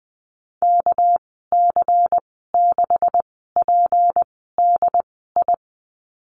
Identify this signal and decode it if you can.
Morse code, heard as a keyed tone, about 20 wpm, spaced as for 14 wpm, 700 Hz. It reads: KC6PDI
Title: 245 Lesson 39g - Letter z - Call Signs 20wpm